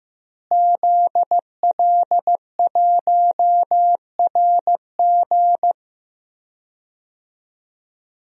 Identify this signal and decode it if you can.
Morse code, heard as a keyed tone, about 15 wpm, 700 Hz. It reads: ZL1RG
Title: DXCC-100 Call Signs - 5 Characters - Repeated 2x - 15wpm